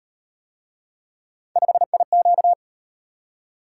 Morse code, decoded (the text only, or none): 5IQ